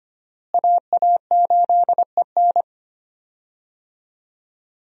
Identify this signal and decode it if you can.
Morse code, heard as a keyed tone, about 25 wpm, 700 Hz. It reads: AA8EN
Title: USA Call Signs - 5 Characters - Repeated 2x - 25wpm